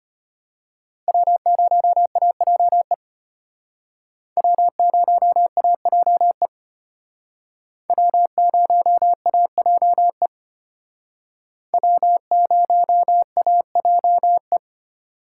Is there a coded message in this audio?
Yes: Morse code from 1.1 to 14.6 s, heard as a keyed tone, about 32 wpm, 700 Hz. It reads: W0AJE W0AJE W0AJE W0AJE